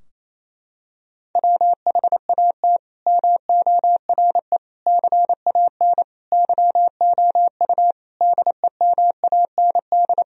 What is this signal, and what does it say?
Morse code, heard as a keyed tone, about 28 wpm, 700 Hz. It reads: WHAT MORE CAN YOU DEMAND
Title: Sentences from Top 1000 Words - Review Part 4 - 28wpm